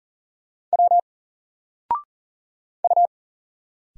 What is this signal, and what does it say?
Morse code, heard as a keyed tone, about 40 wpm, 700 Hz. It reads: W U
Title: Single Character (Letter:Number:Punctuation:Prosign) 40wpm